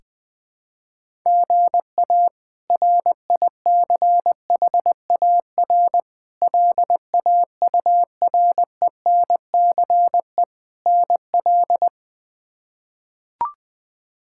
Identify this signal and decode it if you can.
Morse code, heard as a keyed tone, about 20 wpm, 700 Hz. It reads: GA RICHAR LAURENCE NL
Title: K1USN SST QSO Practice - 20wpm